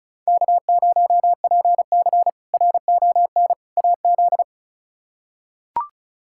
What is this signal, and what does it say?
Morse code, heard as a keyed tone, about 36 wpm, 700 Hz. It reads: K0PC RON AZ
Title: K1USN SST QSO Practice - 35wpm